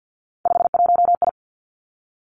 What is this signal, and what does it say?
Morse code, heard as a keyed tone, about 50 wpm, 700 Hz. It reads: 51I
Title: Three Challenging Letter-Number Combinations 50wpm